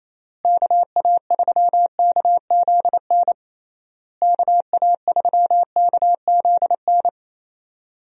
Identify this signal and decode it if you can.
Morse code, heard as a keyed tone, about 28 wpm, 700 Hz. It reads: KA3KZN KA3KZN
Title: USA Call Signs - 6 Characters - Repeated 3x - 28wpm